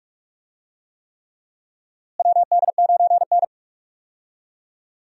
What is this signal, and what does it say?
Morse code, heard as a keyed tone, about 41 wpm, 700 Hz. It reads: WD9N
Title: USA Call Signs - 4 Characters - Repeated 3x - 45wpm